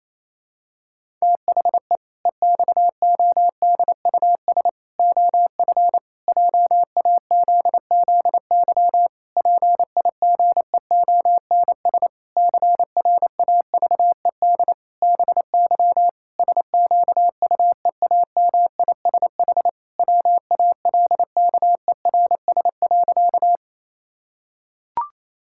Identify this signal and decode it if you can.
Morse code, heard as a keyed tone, about 28 wpm, 700 Hz. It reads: THE EXODUS OF JAZZY PIGEONS CRAVED BY SQUEAMISH WALKERS.
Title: Pangram Challenge v2 - 28wpm